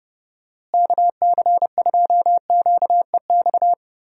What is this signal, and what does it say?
Morse code, heard as a keyed tone, about 32 wpm, 700 Hz. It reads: KC2QEX
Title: USA Call Signs - 6 Characters - Repeated 2x - 30wpm